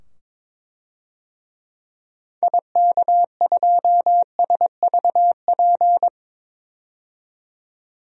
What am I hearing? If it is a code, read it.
Morse code, heard as a keyed tone, about 22 wpm, 700 Hz. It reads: IK2SVP
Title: DXCC-100 Call Signs - 6 Characters - 22wpm